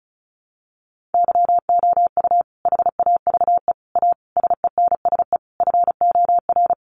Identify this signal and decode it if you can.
Morse code, heard as a keyed tone, about 34 wpm, 700 Hz. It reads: YOU HAVE A SENSE FOR
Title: Sentences from Top 700 Words 35wpm